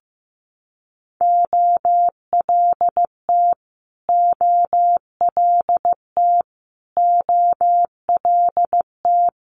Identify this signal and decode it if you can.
Morse code, heard as a keyed tone, about 15 wpm, 700 Hz. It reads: OLT OLT OLT
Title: Multi-letter Phonemes Repeated 3x 15wpm